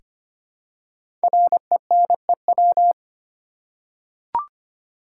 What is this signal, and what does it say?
Morse code, heard as a keyed tone, about 25 wpm, 700 Hz. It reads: RENEW
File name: 5 Letter Words - First 3 Letters Given - 25wpm